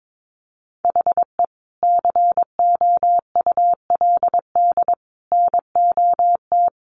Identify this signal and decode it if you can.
Morse code, heard as a keyed tone, about 22 wpm, 700 Hz. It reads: HE COULD NOT
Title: Sentences from Top 1500 Words 22wpm